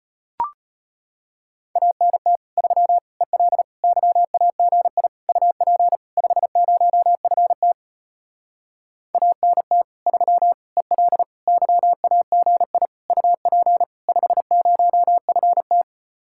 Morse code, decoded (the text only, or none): ANT 3 EL YAGI UP 50FT ANT 3 EL YAGI UP 50FT